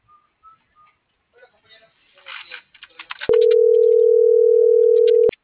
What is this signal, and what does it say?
Ringback tone (North America), 440 + 480 Hz.
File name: CENTRAL MULTIMODAL / out rodolfo.fuentes - .59624